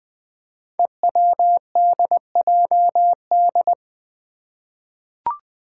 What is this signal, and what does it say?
Morse code, heard as a keyed tone, about 20 wpm, 700 Hz. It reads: EWDJD
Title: Five Challenging Letter-Number Combinations 20wpm